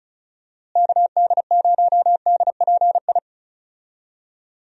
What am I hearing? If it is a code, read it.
Morse code, heard as a keyed tone, about 34 wpm, 700 Hz. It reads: KD0DPI
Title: USA Call Signs - 6 Characters - Repeated 2x - 35wpm